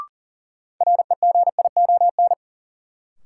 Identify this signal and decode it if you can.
Morse code, heard as a keyed tone, about 40 wpm, 700 Hz. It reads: REGION
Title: Sentences from Top 1500 Words - New Word Drill Alt - 40wpm